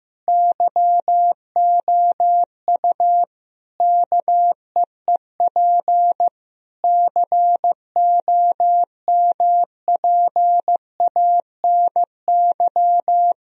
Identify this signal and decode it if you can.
Morse code, heard as a keyed tone, about 16 wpm, 700 Hz. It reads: YOU KEEP COMPANY
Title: Sentences from Top 1000 Words - Review Part 3 - 15wpm